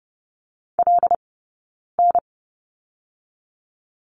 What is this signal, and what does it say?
Morse code, heard as a keyed tone, about 30 wpm, 700 Hz. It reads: L N
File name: course / 30:10wpm / 086 Lesson 15d - Number 5 - Two Characters 30wpm